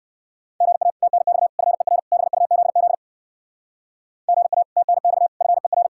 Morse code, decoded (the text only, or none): GA TNX FER BUZZ GA TNX FER